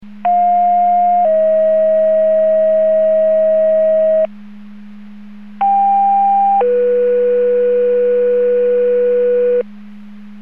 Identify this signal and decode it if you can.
Two-tone page, heard as a one-second tone, then a three-second tone, about 710 then 640 Hz, about 790 then 490 Hz. It is heard from 0.2 to 9.6 s.
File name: Portage 154.385 Paging